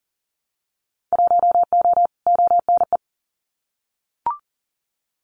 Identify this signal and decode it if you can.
Morse code, heard as a keyed tone, about 42 wpm, 700 Hz. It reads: 1O ONE
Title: Winter Field Day Exchanges 40wpm